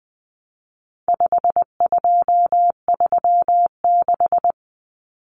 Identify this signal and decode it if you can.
Morse code, heard as a keyed tone, about 20 wpm, 700 Hz. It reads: 5236
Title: Numbers - 4-Digit 20wpm